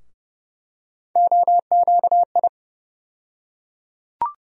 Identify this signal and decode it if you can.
Morse code, heard as a keyed tone, about 30 wpm, 700 Hz. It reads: OQI
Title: USA Call Sign Postfixes - 3 Letters 30wpm